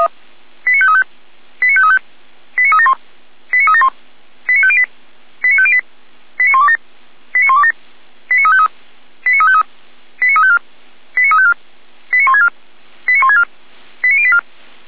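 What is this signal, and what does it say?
Five-tone selective call (ZVEI), heard as quick runs of five tones, tones of about 70 ms. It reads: 89535 89535 89371 89371 89508 89508 89127 89127 89353 89353 89354 89354 89164 89164 89084
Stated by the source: ZVEI five-tone selective call 89167 received with an SDR, hosted by the Feuerwehr Orlamünde fire brigade.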